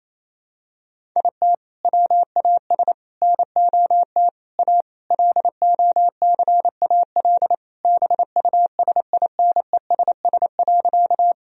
Morse code, heard as a keyed tone, about 27 wpm, 700 Hz. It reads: IT WAS NOT A LOCAL BUSINESS.
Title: Sentences from Top 1000 Words - Review Part 3 - 28wpm